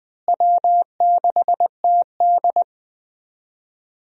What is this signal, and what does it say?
Morse code, heard as a keyed tone, about 20 wpm, 700 Hz. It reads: W6TD
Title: USA Call Signs - 4 Characters - Repeated 2x - 20wpm